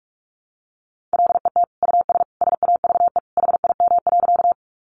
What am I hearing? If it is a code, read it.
Morse code, heard as a keyed tone, about 44 wpm, 700 Hz. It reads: LET US SAVE HIM.
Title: Sentences from Top 1100 Words 45wpm